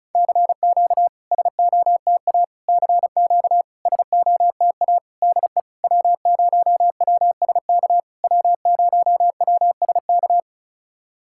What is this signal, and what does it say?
Morse code, heard as a keyed tone, about 34 wpm, 660 Hz. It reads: CQ SOTA CQ SOTA DE W0WSK W0WSK
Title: SOTA QSO Practice 35wpm